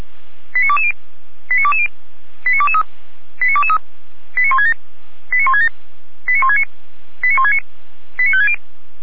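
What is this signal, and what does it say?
Five-tone selective call (ZVEI), heard as quick runs of five tones, tones of about 70 ms. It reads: 89220 89220 89223 89223 89167 89167 89169 89169 89560
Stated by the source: ZVEI five-tone selective call received by SDR.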